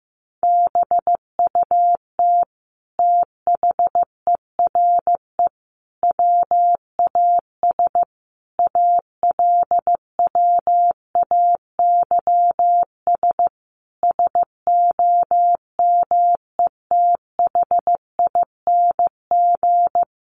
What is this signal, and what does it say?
Morse code, heard as a keyed tone, about 15 wpm, 700 Hz. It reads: BUT THERE WAS ALWAYS SOMETHING
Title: Sentences from Top 900 Words 15wpm